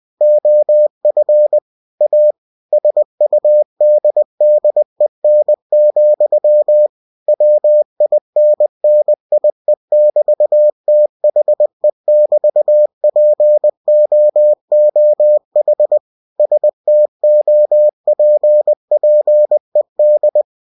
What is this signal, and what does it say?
Morse code, heard as a keyed tone, about 20 wpm, 600 Hz. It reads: OF A SUDDEN, WINNIE<BT>THE<BT>POOH STOPPED
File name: cw-books / winnie-the-pooh / winnie-20wpm-page-0023